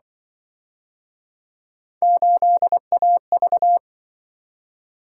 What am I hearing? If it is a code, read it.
Morse code, heard as a keyed tone, about 26 wpm, 700 Hz. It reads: 8AV